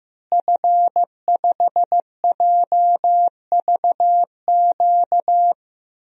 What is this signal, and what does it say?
Morse code, heard as a keyed tone, about 15 wpm, 700 Hz. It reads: F5JVQ
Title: DXCC-100 Call Signs - 5 Characters - Repeated 2x - 15wpm